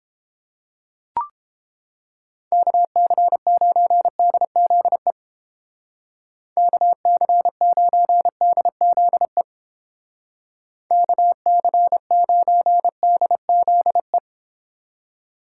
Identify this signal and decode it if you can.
Morse code, heard as a keyed tone, about 32 wpm, 700 Hz. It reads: KC9DZE KC9DZE KC9DZE